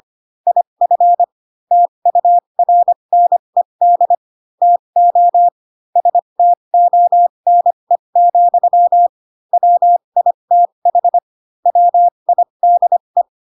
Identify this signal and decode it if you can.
Morse code, heard as a keyed tone, about 25 wpm, spaced as for 22 wpm, 700 Hz. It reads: IF TURNED TO STONE, WITH WIDE